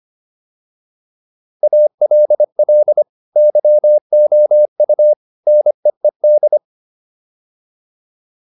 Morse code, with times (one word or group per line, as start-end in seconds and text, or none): 1.6-3.0 ALL
3.4-5.1 YOU
5.5-6.6 NEED